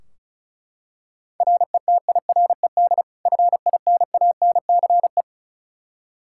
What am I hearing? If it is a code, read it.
Morse code, heard as a keyed tone, about 36 wpm, 700 Hz. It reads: RETIRED FINANCE
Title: Practice QSO - Exchange 3 - More Info - 35wpm